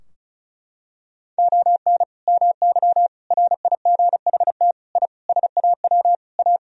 Morse code, heard as a keyed tone, about 36 wpm, 700 Hz. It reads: ON MY RIGHT I SAW A
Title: Sentences from Top 1000 Words - Review Part 2 - 35wpm